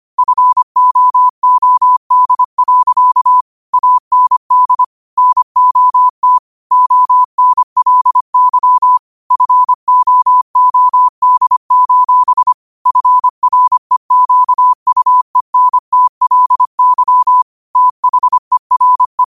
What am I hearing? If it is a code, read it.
Morse code, heard as a keyed tone, about 25 wpm, 1000 Hz. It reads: FOOD. AND NOT ONLY FOOD: FREQUENTLY THERE
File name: qotdfiles / 25wpm